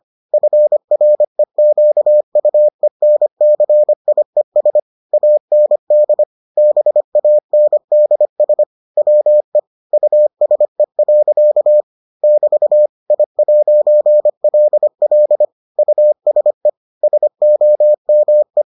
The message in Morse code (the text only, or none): FREQUENCIES AND BANDS WE USE. <BT> I'LL USE SOME